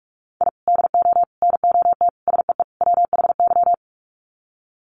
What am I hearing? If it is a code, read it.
Morse code, heard as a keyed tone, about 44 wpm, 700 Hz. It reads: I DO NOT SEE WHY
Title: Sets of 5 Words Encore 45wpm